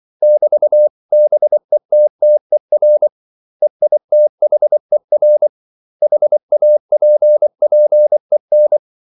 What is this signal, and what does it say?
Morse code, heard as a keyed tone, about 24 wpm, 600 Hz. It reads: <BT>BETTER EITHER HAPPEN
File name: CW / Practice / 500MostCommonEnglishWords24